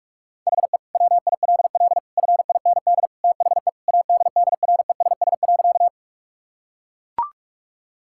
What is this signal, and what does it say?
Morse code, heard as a keyed tone, about 42 wpm, 690 Hz. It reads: HE WILL FIND THE ADDRESS.
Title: Sentences from Top 1000 Words - Review Part 6 - 45wpm